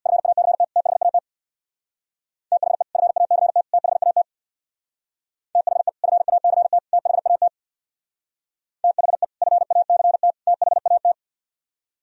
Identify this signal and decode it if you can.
Morse code, heard as a keyed tone, about 61 wpm, 700 Hz. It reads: FACT THAT THE FACT THAT THE FACT THAT THE FACT THAT
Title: Speed-Racing - Sets of 3 Words 50wpm